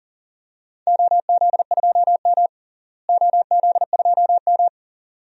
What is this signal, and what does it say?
Morse code, heard as a keyed tone, about 40 wpm, 700 Hz. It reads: OZ2M OZ2M
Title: DXCC-100 Call Signs - 4 Characters - Repeated 2x - 40wpm